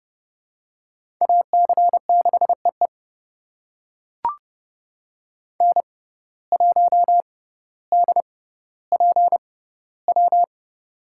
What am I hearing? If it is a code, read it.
Morse code, heard as a keyed tone, about 30 wpm, spaced as for 15 wpm, 700 Hz. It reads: AC6EE N1DPW
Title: USA Call Signs - 5 Characters - 6x Character Spacing - 30wpm